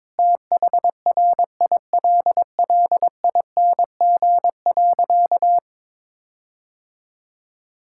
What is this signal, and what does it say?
Morse code, heard as a keyed tone, about 22 wpm, 700 Hz. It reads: THRILLING.
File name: Difficult Sentences with Hints - 22wpm